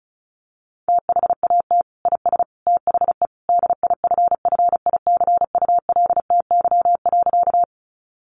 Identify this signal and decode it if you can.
Morse code, heard as a keyed tone, about 34 wpm, 700 Hz. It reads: THAT IS THE DIFFICULTY.
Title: Sentences from Top 1500 Words 35wpm